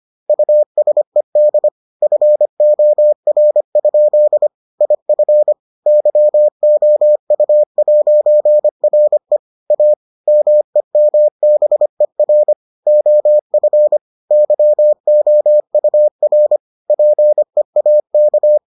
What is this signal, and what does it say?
Morse code, heard as a keyed tone, about 25 wpm, 600 Hz. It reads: USED FOR? IF YOU'RE A MEMBER OF YOUR PEAK